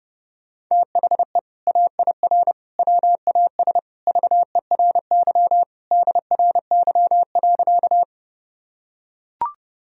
Morse code, heard as a keyed tone, about 30 wpm, 700 Hz. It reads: THE AIR WAS VERY DRY.
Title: Sentences from Top 1000 Words - Review Part 3 - 30wpm